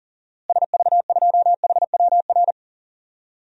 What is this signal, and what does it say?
Morse code, heard as a keyed tone, about 40 wpm, 700 Hz. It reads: SV2HWR
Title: DXCC-100 Call Signs - 6 Characters - Repeated 2x - 40wpm